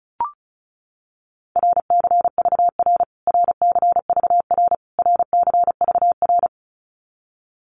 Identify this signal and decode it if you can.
Morse code, heard as a keyed tone, about 34 wpm, 700 Hz. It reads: RCVR RCVR RCVR
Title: Abbreviations Repeated 3x - 35wpm